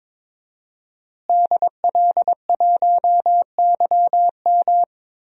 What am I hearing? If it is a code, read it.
Morse code, heard as a keyed tone, about 22 wpm, 700 Hz. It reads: DL1YM
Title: DXCC-100 Call Signs - 5 Characters - Repeated 2x - 22wpm